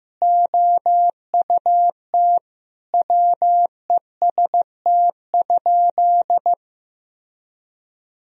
Morse code, heard as a keyed tone, about 15 wpm, 700 Hz. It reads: OUT WEST?
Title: Sentences from Top 1000 Words - Review Part 2 - 15wpm